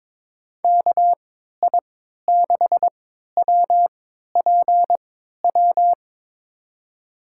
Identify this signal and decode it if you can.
Morse code, heard as a keyed tone, about 22 wpm, spaced as for 12 wpm, 700 Hz. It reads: KI6WPW
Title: USA Call Signs - 3x Character Spacing - 22wpm